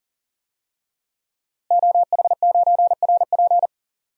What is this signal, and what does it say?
Morse code, heard as a keyed tone, about 40 wpm, 700 Hz. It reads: OH9RP
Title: DXCC-100 Call Signs - 5 Characters - Repeated 2x - 40wpm